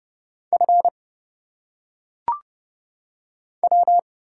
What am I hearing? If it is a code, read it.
Morse code, heard as a keyed tone, about 31 wpm, 700 Hz. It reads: F W
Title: Single Letters - Phonetic Pronunciation 30wpm